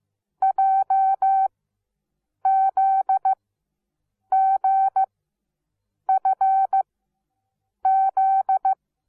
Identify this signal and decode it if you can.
Morse code, heard as a keyed tone, about 15 wpm, spaced as for 7 wpm, 770 Hz. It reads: JZGFZ
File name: Onlinelehrgang / morsen / DJ4UF Morsen17